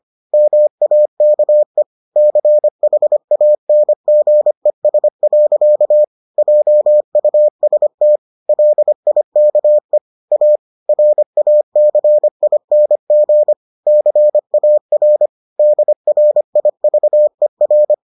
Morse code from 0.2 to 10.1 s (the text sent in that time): MAKE CHANGES. JUST LIKE